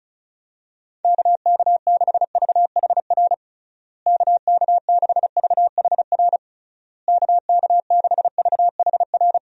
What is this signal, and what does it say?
Morse code, heard as a keyed tone, about 36 wpm, 700 Hz. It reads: KK6VHR KK6VHR KK6VHR
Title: USA Call Signs - 6 Characters - Repeated 3x - 35wpm